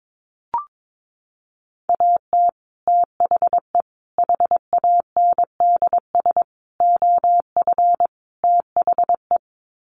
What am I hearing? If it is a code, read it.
Morse code, heard as a keyed tone, about 22 wpm, 700 Hz. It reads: AT THE HANDS OF THE
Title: Sets of 5 Words Encore2 22wpm